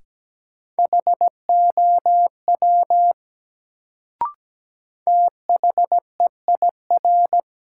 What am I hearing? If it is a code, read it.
Morse code, heard as a keyed tone, about 17 wpm, 700 Hz. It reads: HOW THEIR
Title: Sets of 2 Words - New Word Drill - 17wpm